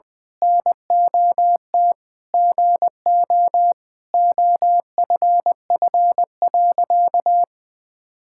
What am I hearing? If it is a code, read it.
Morse code, heard as a keyed tone, about 21 wpm, 700 Hz. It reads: NOT GO OFF.
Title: Sentences with 4-Word Phrases - 20wpm